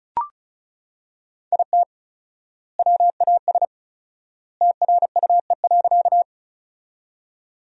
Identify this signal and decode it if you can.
Morse code, heard as a keyed tone, about 36 wpm, 700 Hz. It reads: IT WAS TRUE.
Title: 4x Word-Spacing - Sentences from Top 400 Words - 35wpm